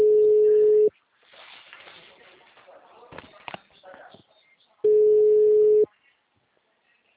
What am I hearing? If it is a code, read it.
Ringback tone (Europe, 425 Hz), with speech.